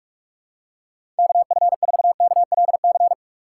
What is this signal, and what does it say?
Morse code, heard as a keyed tone, about 45 wpm, 700 Hz. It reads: KF4KLC
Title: USA Call Signs - 6 Characters - 2x Character Spacing - 45wpm